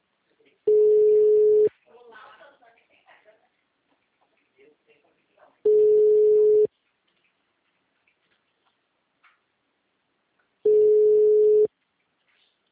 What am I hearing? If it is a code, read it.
Ringback tone (Europe, 425 Hz), with speech.
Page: Campinas - SP